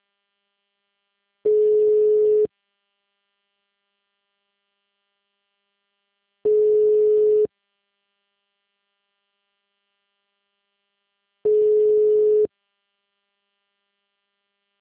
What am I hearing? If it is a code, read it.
Ringback tone (Europe, 425 Hz).